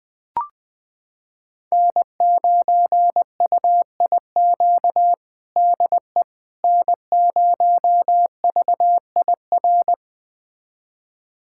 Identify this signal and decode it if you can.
Morse code, heard as a keyed tone, about 20 wpm, 700 Hz. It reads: N9UIQ DE N0VIR
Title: USA Callsign DE Callsign - 5 Characters - 20wpm